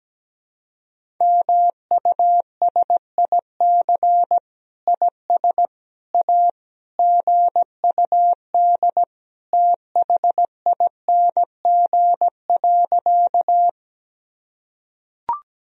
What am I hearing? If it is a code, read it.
Morse code, heard as a keyed tone, about 17 wpm, 700 Hz. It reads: MUSIC IS A GUD THING.
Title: Sentences from Top 300 Words and Abbreviations 17wpm